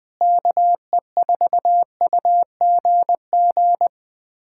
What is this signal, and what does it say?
Morse code, heard as a keyed tone, about 20 wpm, 700 Hz. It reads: KE4UGG
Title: USA Callsign DE Callsign - 6 Characters - 20wpm